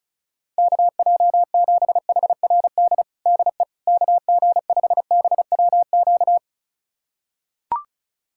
Morse code, heard as a keyed tone, about 34 wpm, 700 Hz. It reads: KJ7HRD DE KG5BWQ
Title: USA Callsign DE Callsign 35wpm